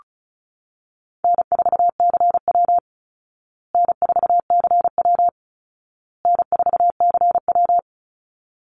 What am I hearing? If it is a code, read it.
Morse code, heard as a keyed tone, about 34 wpm, 700 Hz. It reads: N4CW N4CW N4CW
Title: Call Signs Repeated 2x - 35wpm